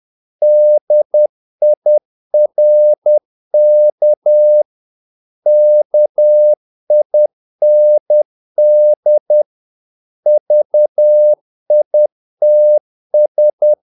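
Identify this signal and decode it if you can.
Morse code, heard as a keyed tone, about 10 wpm, 600 Hz. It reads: DIRK KIND VITS
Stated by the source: Morse practice of short Danish words at 10 wpm, from CW med Gnister.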